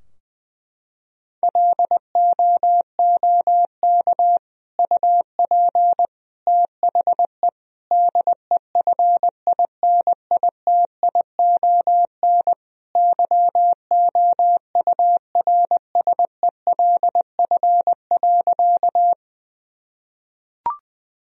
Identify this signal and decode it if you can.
Morse code, heard as a keyed tone, about 20 wpm, 700 Hz. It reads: LOOK UP THE DEFINITION YOURSELF.
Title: Sentences from Top 1000 Words - Review Part 4 - 20wpm